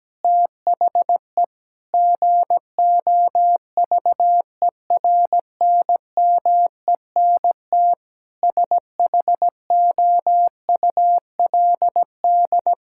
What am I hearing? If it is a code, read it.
Morse code, heard as a keyed tone, about 17 wpm, 700 Hz. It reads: THE GOVERNMENT SHOULD
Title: Sentences from Top 1000 Words - Review Part 6 - 17wpm